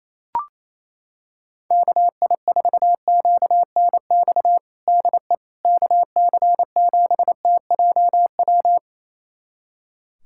Morse code, heard as a keyed tone, about 28 wpm, 700 Hz. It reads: KI4QNX DE KC7TJW
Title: USA Callsign DE Callsign - 6 Characters - 28wpm